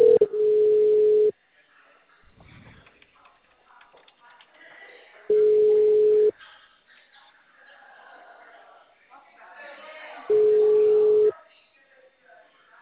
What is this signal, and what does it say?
Ringback tone (Europe, 425 Hz), with speech.